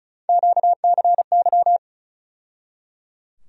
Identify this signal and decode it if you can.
Morse code, heard as a keyed tone, about 34 wpm, 700 Hz. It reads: QCY
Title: USA Call Sign Postfixes - 3 Letters 35wpm